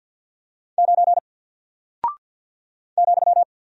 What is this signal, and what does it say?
Morse code, heard as a keyed tone, about 48 wpm, 700 Hz. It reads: 9 ,
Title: Single Character (Letter:Number:Punctuation:Prosign) Alt 50wpm